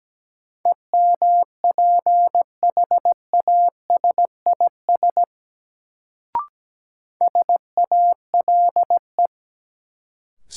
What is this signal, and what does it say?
Morse code, heard as a keyed tone, about 17 wpm, 700 Hz. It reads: EMPHASIS SALE
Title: Sentences from Top 1400 Words - New Word Drill Alt - 17wpm